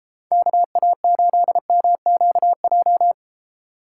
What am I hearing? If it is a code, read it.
Morse code, heard as a keyed tone, about 31 wpm, 700 Hz. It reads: KA8MQJ